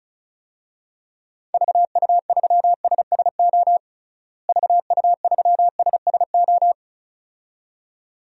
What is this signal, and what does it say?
Morse code, heard as a keyed tone, about 36 wpm, 700 Hz. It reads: VU3SSO VU3SSO
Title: DXCC-100 Call Signs - 6 Characters - Repeated 2x - 35wpm